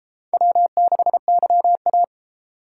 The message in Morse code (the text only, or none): W6YA